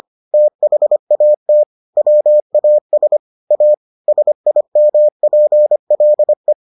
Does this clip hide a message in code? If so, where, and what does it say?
Yes: Morse code from 0.3 to 6.5 s, heard as a keyed tone, about 25 wpm, 600 Hz. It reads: THAT WAS A SIMPLE